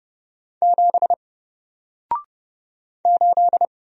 Morse code, heard as a keyed tone, about 30 wpm, 700 Hz. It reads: 7 8
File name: Single-Letter-Number-30wpm